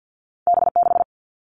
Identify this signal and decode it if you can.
Morse code, heard as a keyed tone, about 50 wpm, 700 Hz. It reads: B 6
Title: All Cut Numbers - 2-Digit 50wpm